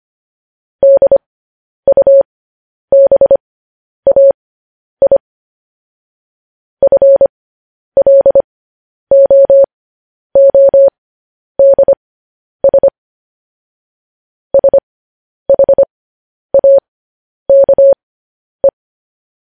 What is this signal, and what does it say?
Morse code, heard as a keyed tone, about 25 wpm, spaced as for 10 wpm, 550 Hz. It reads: DUBAI FLOODS SHAKE
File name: podcast / News headlines-1x-25x10